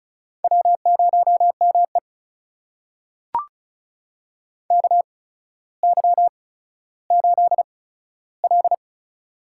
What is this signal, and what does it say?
Morse code, heard as a keyed tone, about 36 wpm, spaced as for 15 wpm, 700 Hz. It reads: W0ME KY8L
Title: USA Call Signs - 4 Characters - 8x Character Spacing - 35wpm